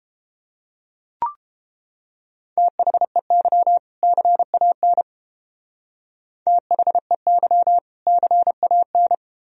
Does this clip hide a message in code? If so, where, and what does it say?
Yes: Morse code from 2.6 to 9.2 s, heard as a keyed tone, about 30 wpm, 700 Hz. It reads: THEY CAN THEY CAN